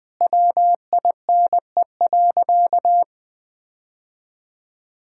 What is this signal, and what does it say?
Morse code, heard as a keyed tone, about 20 wpm, 700 Hz. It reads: WINE.
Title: All-Age Appropriate Jokes - 20wpm